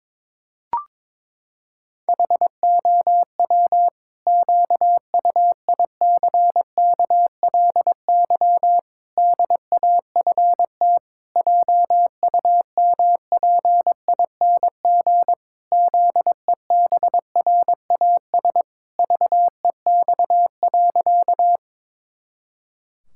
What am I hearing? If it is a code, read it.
Morse code, heard as a keyed tone, about 21 wpm, 700 Hz. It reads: HOW QUICKLY DAFT JUMPING ZEBRAS VEX.
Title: Pangram Challenge v2 - 22wpm